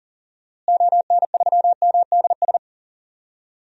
Morse code, heard as a keyed tone, about 40 wpm, 700 Hz. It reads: ON3MDS
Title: DXCC-100 Call Signs - 6 Characters - Repeated 2x - 40wpm